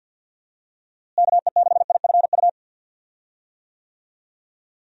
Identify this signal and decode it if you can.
Morse code, heard as a keyed tone, about 50 wpm, 700 Hz. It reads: KE6IFU
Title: USA Call Signs - 6 Characters - Repeated 2x - 50wpm